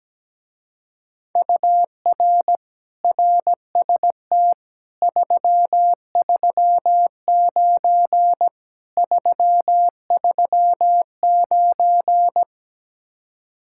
Morse code, heard as a keyed tone, about 17 wpm, 700 Hz. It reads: UR RST 339 339
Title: Speed-Racing Common QSO Phrases - 17wpm